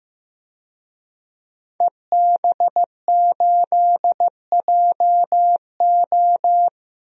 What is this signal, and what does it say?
Morse code, heard as a keyed tone, about 15 wpm, 700 Hz. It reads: EB8JO
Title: DXCC-100 Call Signs - 5 Characters - Repeated 2x - 15wpm